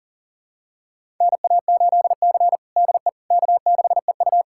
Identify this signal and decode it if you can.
Morse code, heard as a keyed tone, about 40 wpm, 700 Hz. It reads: NA8C DE K6EU
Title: USA Callsign DE Callsign - 4 Characters - 40wpm